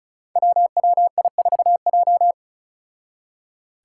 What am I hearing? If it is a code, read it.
Morse code, heard as a keyed tone, about 36 wpm, 700 Hz. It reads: WWI4J
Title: Five Challenging Letter-Number Combinations 35wpm